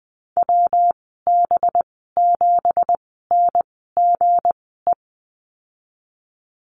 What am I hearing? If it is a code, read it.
Morse code, heard as a keyed tone, about 20 wpm, spaced as for 15 wpm, 700 Hz. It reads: WB7NGE
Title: 186 Lesson 30g - Letter b - Call Signs 20wpm